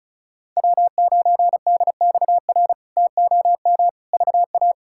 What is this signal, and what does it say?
Morse code, heard as a keyed tone, about 36 wpm, 700 Hz. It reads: W9DXR TOM VA
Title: K1USN SST QSO Practice - 35wpm